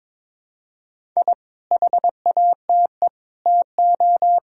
Morse code, heard as a keyed tone, about 22 wpm, 700 Hz. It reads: I HATE TO
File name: practice / 22wpm / Sets of 8 Words - 22wpm